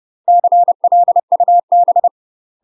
Morse code, heard as a keyed tone, about 30 wpm, 700 Hz. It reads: CLUB
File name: UK MASTER QSOs TRAINING / 30WPM / UK MASTER QSOs 0028